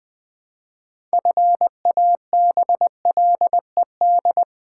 Morse code, heard as a keyed tone, about 20 wpm, 700 Hz. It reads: FABLED
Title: Pangram Challenge v2 - 20wpm